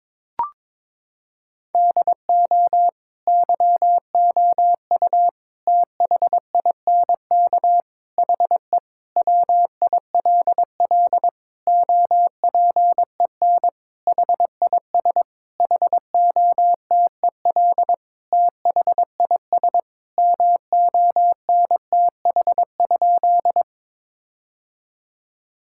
Morse code, heard as a keyed tone, about 22 wpm, 700 Hz. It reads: DO YOU THINK HE WILL OPEN HIS HOTEL THIS MONTH?
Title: Sentences from Top 1600 Words 22wpm